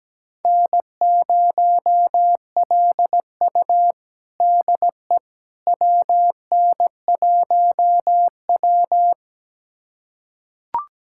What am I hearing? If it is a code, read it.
Morse code, heard as a keyed tone, about 17 wpm, 700 Hz. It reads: N0LU DE WN1W
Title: USA Callsign DE Callsign - 4 Characters - 17wpm